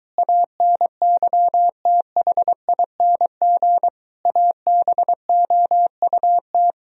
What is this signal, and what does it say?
Morse code, heard as a keyed tone, about 22 wpm, 700 Hz. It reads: ANYTHING ABOUT